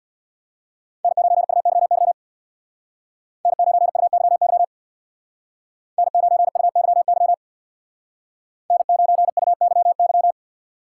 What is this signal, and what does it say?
Morse code, heard as a keyed tone, about 64 wpm, 700 Hz. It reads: N9UYY N9UYY N9UYY N9UYY